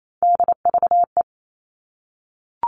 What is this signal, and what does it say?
Morse code, heard as a keyed tone, about 28 wpm, 700 Hz. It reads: DVE
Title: All Cut Numbers - 3-Digit 28wpm